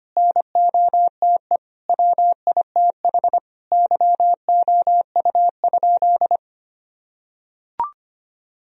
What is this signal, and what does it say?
Morse code, heard as a keyed tone, about 25 wpm, 700 Hz. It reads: NOTE WITH YOU?
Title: Sentences from Top 1000 Words - Review Part 2 - 25wpm